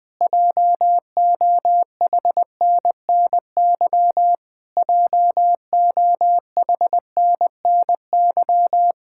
Morse code, heard as a keyed tone, about 20 wpm, 700 Hz. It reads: JOHNNY JOHNNY
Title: Practice QSO - Exchange 1 - Beginning - 20wpm